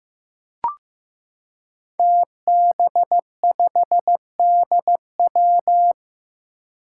Morse code, heard as a keyed tone, about 15 wpm, 700 Hz. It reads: TB5DW
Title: Five Challenging Letter-Number Combinations 15wpm